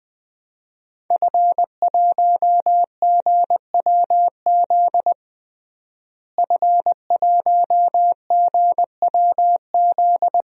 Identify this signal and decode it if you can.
Morse code, heard as a keyed tone, about 20 wpm, 700 Hz. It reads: F1GWZ F1GWZ
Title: DXCC-100 Call Signs - 5 Characters - Repeated 2x - 20wpm